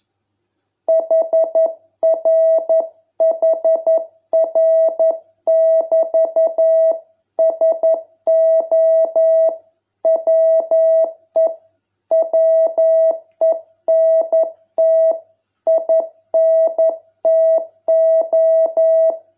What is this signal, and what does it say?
Morse code, heard as a keyed tone, about 11 wpm, 660 Hz. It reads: HRHR<BT>SO WE WENT INTO